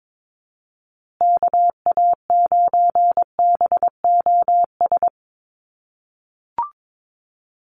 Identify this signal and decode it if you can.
Morse code, heard as a keyed tone, about 22 wpm, 700 Hz. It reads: KA9BOS
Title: USA Call Signs Repeated 3x 22wpm